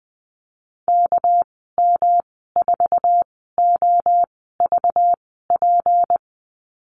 Morse code, heard as a keyed tone, about 20 wpm, spaced as for 15 wpm, 700 Hz. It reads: KM4OVP
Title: 219 Lesson 35g - Number 0 - Call Signs 20wpm